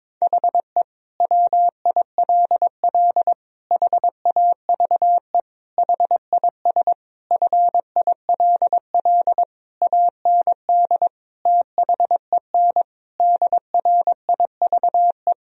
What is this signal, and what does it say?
Morse code, heard as a keyed tone, about 22 wpm, 700 Hz. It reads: HE WILL HAVE HIS FILL AND THEN DRIVE
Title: Sentences from Top 1000 Words - Review Part 3 - 22wpm